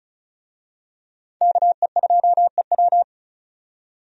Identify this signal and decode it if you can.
Morse code, heard as a keyed tone, about 36 wpm, 700 Hz. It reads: KE2EW
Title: USA Call Signs - 5 Characters - 35wpm